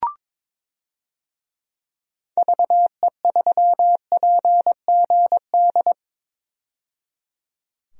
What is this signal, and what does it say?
Morse code, heard as a keyed tone, about 21 wpm, 700 Hz. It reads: VE3PGD
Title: DXCC-100 Call Signs - 6 Characters - 22wpm